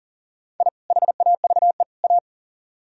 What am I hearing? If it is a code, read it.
Morse code, heard as a keyed tone, about 54 wpm, spaced as for 47 wpm, 700 Hz. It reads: I HAVE A
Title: Speed-Racing - Sets of 3 Words 40wpm